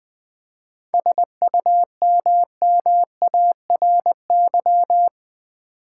Morse code, heard as a keyed tone, about 20 wpm, 700 Hz. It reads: SUMMARY
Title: 7 Letter Words - First 3 Letters Given - 20wpm